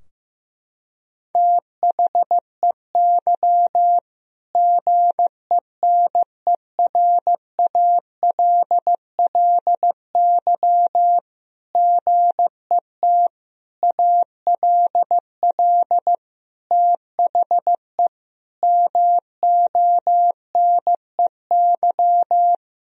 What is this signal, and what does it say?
Morse code, heard as a keyed tone, about 15 wpm, 700 Hz. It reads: THEY GENERALLY GET ALL THE MONEY
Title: Sentences from Top 1000 Words - Review Part 4 - 15wpm